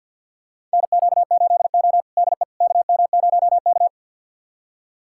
Morse code, heard as a keyed tone, about 46 wpm, 700 Hz. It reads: NQ8O DE KM0K
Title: USA Callsign DE Callsign - 4 Characters - 50wpm